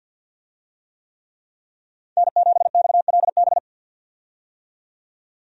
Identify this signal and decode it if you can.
Morse code, heard as a keyed tone, about 51 wpm, 700 Hz. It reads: N7XLB
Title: USA Call Signs - 5 Characters - 50wpm